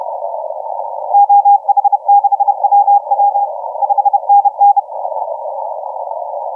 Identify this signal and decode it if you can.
Morse code, heard as a keyed tone, about 30 wpm, 810 Hz. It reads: OH6WW 5NN
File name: VU7AG 0636UTC 10m